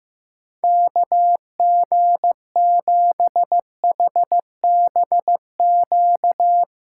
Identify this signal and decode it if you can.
Morse code, heard as a keyed tone, about 16 wpm, 700 Hz. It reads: KG7HBQ
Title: USA Callsign DE Callsign 15wpm